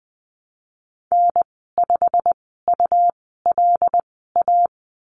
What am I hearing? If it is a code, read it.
Morse code, heard as a keyed tone, about 19 wpm, spaced as for 14 wpm, 700 Hz. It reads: N5ULA
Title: 167 Lesson 27g - Number 7 - Call Signs 20wpm